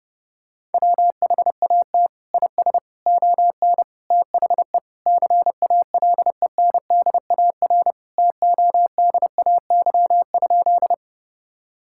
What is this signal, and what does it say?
Morse code, heard as a keyed tone, about 30 wpm, 700 Hz. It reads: WHAT IS ON THE CALENDAR TODAY?
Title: Sentences from Top 1600 Words 30wpm